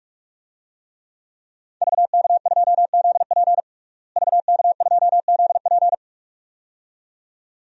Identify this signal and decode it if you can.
Morse code, heard as a keyed tone, about 46 wpm, 700 Hz. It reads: VK2ZP VK2ZP
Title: DXCC-100 Call Signs - 5 Characters - Repeated 2x - 45wpm